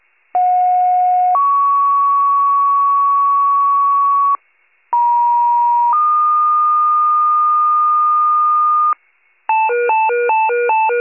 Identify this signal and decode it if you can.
Two-tone page, heard as a one-second tone, then a three-second tone, about 730 then 1120 Hz, about 950 then 1220 Hz. It is heard from 0.3 to 8.9 s.
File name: wcalert